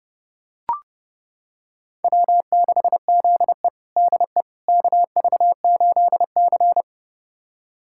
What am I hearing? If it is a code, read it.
Morse code, heard as a keyed tone, about 32 wpm, 700 Hz. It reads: W6ZE DE KV8C
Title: USA Callsign DE Callsign - 4 Characters - 30wpm